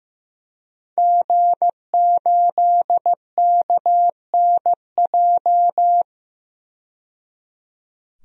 Morse code, heard as a keyed tone, about 15 wpm, 700 Hz. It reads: G8KNJ
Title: DXCC-100 Call Signs - 5 Characters - 15wpm